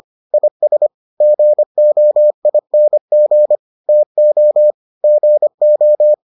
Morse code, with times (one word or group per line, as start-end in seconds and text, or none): 0.3-0.9 IS
1.2-3.6 GOING
3.9-4.7 TO
5.0-6.1 GO